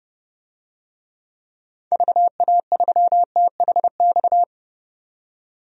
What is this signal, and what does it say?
Morse code, heard as a keyed tone, about 30 wpm, 700 Hz. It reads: VA3THX
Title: DXCC-100 Call Signs - 6 Characters - 30wpm